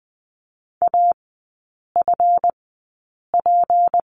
Morse code, heard as a keyed tone, about 20 wpm, spaced as for 8 wpm, 700 Hz. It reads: AFP
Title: 152 Lesson 25e - Letter p - Three Characters 20wpm